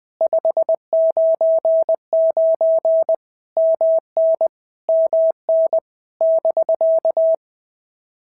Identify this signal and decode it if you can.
Morse code, heard as a keyed tone, about 20 wpm, 650 Hz. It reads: 599 MN MN <BK>
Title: POTA QSO Practice - 20wpm